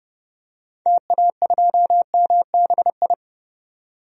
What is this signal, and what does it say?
Morse code, heard as a keyed tone, about 30 wpm, 700 Hz. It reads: TA2MBI
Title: DXCC-100 Call Signs - 6 Characters - 30wpm